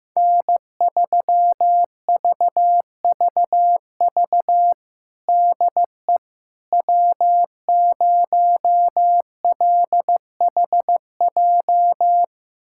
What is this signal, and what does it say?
Morse code, heard as a keyed tone, about 15 wpm, 700 Hz. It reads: N3VVV DE W0LHJ
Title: USA Callsign DE Callsign - 5 Characters - 15wpm